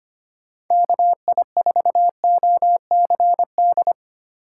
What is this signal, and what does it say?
Morse code, heard as a keyed tone, about 25 wpm, 700 Hz. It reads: KI4OCD
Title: USA Call Signs - 6 Characters - 3x Character Spacing - 25wpm